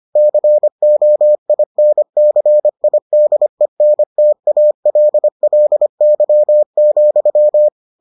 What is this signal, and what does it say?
Morse code, heard as a keyed tone, about 25 wpm, 600 Hz. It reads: COINCIDENTALLY,